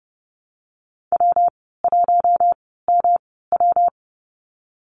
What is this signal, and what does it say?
Morse code, heard as a keyed tone, about 30 wpm, spaced as for 17 wpm, 700 Hz. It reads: W1MW
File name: Lesson 19g - Letter w - Call Signs 30wpm